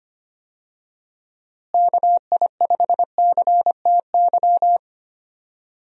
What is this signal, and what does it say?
Morse code, heard as a keyed tone, about 25 wpm, 700 Hz. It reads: KI5CTY
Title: USA Call Signs - 6 Characters - Repeated 3x - 25wpm